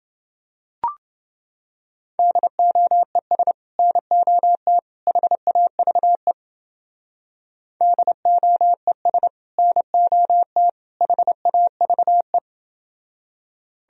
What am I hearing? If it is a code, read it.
Morse code, heard as a keyed tone, about 30 wpm, 700 Hz. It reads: DOES NOT HAVE DOES NOT HAVE